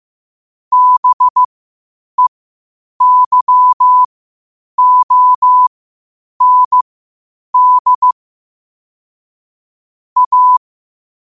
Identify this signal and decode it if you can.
Morse code, heard as a keyed tone, about 15 wpm, spaced as for 8 wpm, 1000 Hz. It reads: BEYOND A
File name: qotdfiles / 8wpm